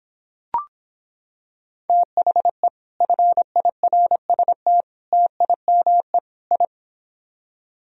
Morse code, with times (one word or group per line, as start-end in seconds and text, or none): 1.9-2.7 THE
3.0-4.8 FIRST
5.1-6.2 TIME
6.5-6.7 I